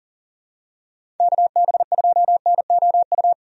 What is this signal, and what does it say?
Morse code, heard as a keyed tone, about 40 wpm, 700 Hz. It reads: KB2NOU
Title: USA Call Signs - 6 Characters - 40wpm